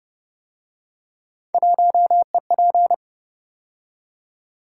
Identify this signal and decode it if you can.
Morse code, heard as a keyed tone, about 31 wpm, 700 Hz. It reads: 1EP